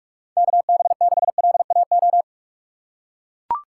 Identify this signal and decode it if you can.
Morse code, heard as a keyed tone, about 42 wpm, 700 Hz. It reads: KB6LAO
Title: USA Call Signs - 6 Characters - 3x Character Spacing - 45wpm